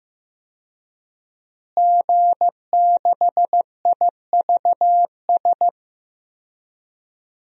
Morse code, heard as a keyed tone, about 15 wpm, 700 Hz. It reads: G6IVS
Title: DXCC-100 Call Signs - 5 Characters - Repeated 3x - 15wpm